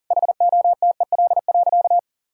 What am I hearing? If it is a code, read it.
Morse code, heard as a keyed tone, about 40 wpm, 700 Hz. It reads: HOTEL.
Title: Sentences from Top 1600 Words 40wpm